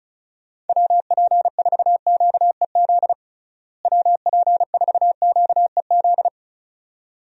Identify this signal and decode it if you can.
Morse code, heard as a keyed tone, about 36 wpm, 700 Hz. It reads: WP4QEZ WP4QEZ
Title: USA Call Signs - 6 Characters - Repeated 2x - 35wpm